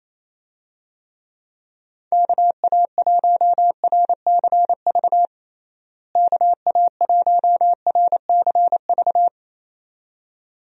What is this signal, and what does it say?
Morse code, heard as a keyed tone, about 27 wpm, 700 Hz. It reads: KA1RCV KA1RCV
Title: USA Call Signs - 6 Characters - Repeated 2x - 28wpm